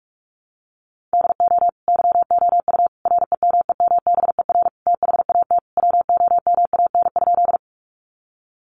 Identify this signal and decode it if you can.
Morse code, heard as a keyed tone, about 44 wpm, 700 Hz. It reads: DO YOU REMEMBER THAT WOMAN?
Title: Sentences from Top 1100 Words 45wpm